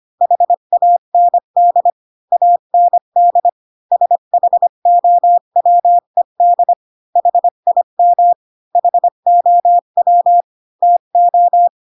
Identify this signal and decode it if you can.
Morse code, heard as a keyed tone, about 25 wpm, 700 Hz. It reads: HAND AND SHOWED HIM HOW TO
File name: Pinocchio / Pinocchio ch3 20 25 ik1hge